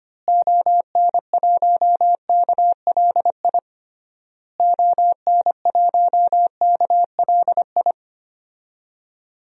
Morse code, heard as a keyed tone, about 25 wpm, 700 Hz. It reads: ON1KLI ON1KLI
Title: DXCC-100 Call Signs - 6 Characters - Repeated 3x - 25wpm